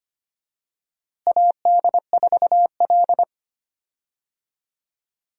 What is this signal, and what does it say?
Morse code, heard as a keyed tone, about 25 wpm, 700 Hz. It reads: AD4L
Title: USA Call Signs - 4 Characters - Repeated 3x - 25wpm